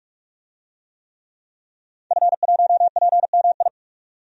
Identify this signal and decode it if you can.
Morse code, heard as a keyed tone, about 41 wpm, 700 Hz. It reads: F1PMI
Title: DXCC-100 Call Signs - 5 Characters - 45wpm